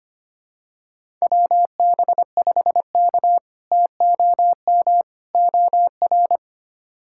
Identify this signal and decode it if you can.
Morse code, heard as a keyed tone, about 25 wpm, 700 Hz. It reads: WB5K TOM OR
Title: K1USN SST QSO Practice - 25wpm